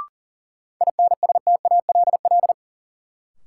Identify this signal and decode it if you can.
Morse code, heard as a keyed tone, about 40 wpm, 700 Hz. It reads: INSTALL
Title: Sentences from Top 1600 Words - New Word Drill Alt - 40wpm